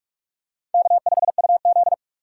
Morse code, heard as a keyed tone, about 44 wpm, 690 Hz. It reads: K5UZ
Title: USA Call Signs - 4 Characters - 2x Character Spacing - 45wpm